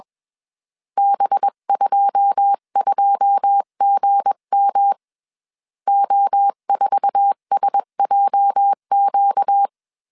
Morse code, heard as a keyed tone, about 21 wpm, 800 Hz. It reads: B22GM O4SJQ
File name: CW / DET / 1 9 1500C S B 20wpm 2313